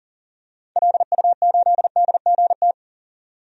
Morse code, heard as a keyed tone, about 40 wpm, 700 Hz. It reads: LU8DGT
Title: DXCC-100 Call Signs - 6 Characters - 40wpm